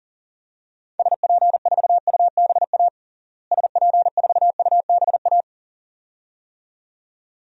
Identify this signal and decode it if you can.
Morse code, heard as a keyed tone, about 40 wpm, 700 Hz. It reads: SP4UBA SP4UBA
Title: DXCC-100 Call Signs - 6 Characters - Repeated 2x - 40wpm